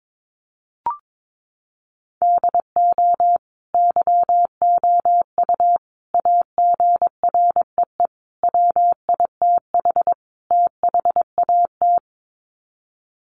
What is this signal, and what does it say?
Morse code, heard as a keyed tone, about 22 wpm, 700 Hz. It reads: DO YOU AGREE WITH THAT
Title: Sets of 5 Words Encore 22wpm